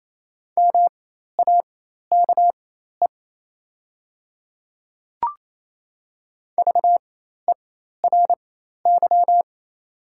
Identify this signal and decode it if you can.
Morse code, heard as a keyed tone, about 27 wpm, spaced as for 13 wpm, 690 Hz. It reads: MAKE VERY
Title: 4x Character Spacing - Top 300 Words - 28wpm